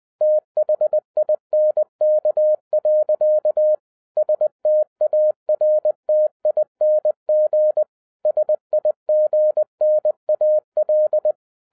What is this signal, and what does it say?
Morse code, heard as a keyed tone, about 20 wpm, 610 Hz. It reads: THINK. STARTING SIGNAL